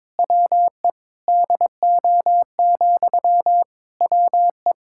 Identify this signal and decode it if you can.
Morse code, heard as a keyed tone, about 22 wpm, 700 Hz. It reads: WE DO, WE
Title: Sentences from Top 1600 Words 22wpm